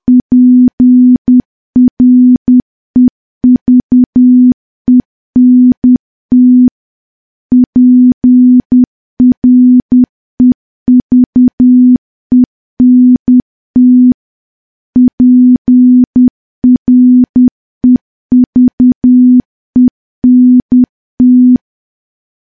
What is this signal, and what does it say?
Morse code, heard as a keyed tone, about 10 wpm, 250 Hz. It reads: PREVENT PREVENT PREVENT
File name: morse / morse-training-10WPM